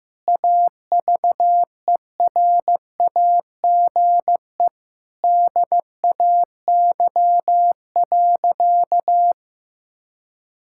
Morse code, heard as a keyed tone, about 15 wpm, 700 Hz. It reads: AVERAGE DAY.
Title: Sentences from Top 1000 Words - Review Part 6 - 15wpm